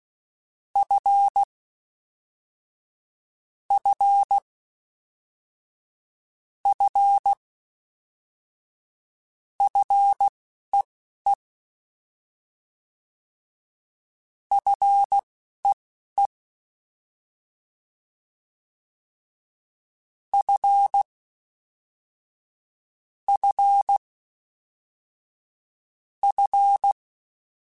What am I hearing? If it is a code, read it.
Morse code, heard as a keyed tone, about 16 wpm, spaced as for 11 wpm, 800 Hz. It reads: F F F FEE FEE F F F
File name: F and L / F and L mp3 / F and L Prefixes 3x2 Twice @10